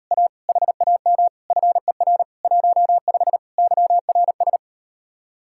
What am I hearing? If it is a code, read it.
Morse code, heard as a keyed tone, about 38 wpm, 700 Hz. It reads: A HAM FER 15 YRS